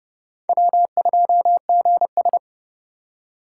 Morse code, heard as a keyed tone, about 26 wpm, 700 Hz. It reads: W2GS